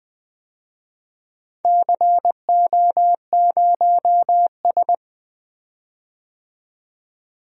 Morse code, heard as a keyed tone, about 21 wpm, 700 Hz. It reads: CO0S
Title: DXCC-100 Call Signs - 4 Characters - 20wpm